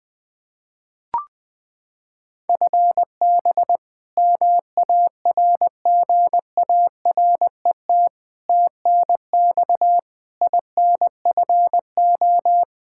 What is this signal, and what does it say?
Morse code, heard as a keyed tone, about 21 wpm, 700 Hz. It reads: FB MARGARET TNX INFO
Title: Practice QSO - Exchange 3 - More Info - 20wpm